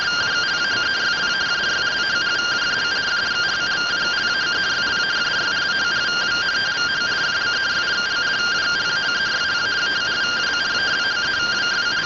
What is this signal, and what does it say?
Two-tone teleprinter signal, about 170 Hz shift.